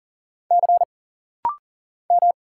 Morse code, heard as a keyed tone, about 40 wpm, 700 Hz. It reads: C M
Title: Single Letter-Number Alt - Rapid-Fire 40wpm